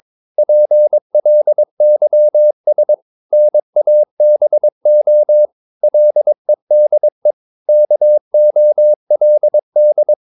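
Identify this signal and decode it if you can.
Morse code, heard as a keyed tone, about 22 wpm, 600 Hz. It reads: PLYS NABO LEDE KOLD
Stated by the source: Morse practice of short Danish words at 22 wpm, from CW med Gnister.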